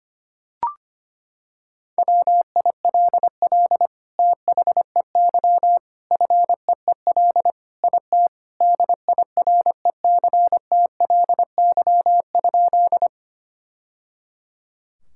Morse code, heard as a keyed tone, about 25 wpm, 700 Hz. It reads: WILL THEY FEEL IT DIRECTLY?
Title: Sentences from Top 1000 Words - Review Part 5 - 25wpm